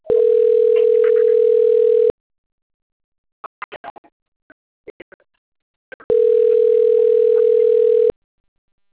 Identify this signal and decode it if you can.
Ringback tone (North America), 440 + 480 Hz.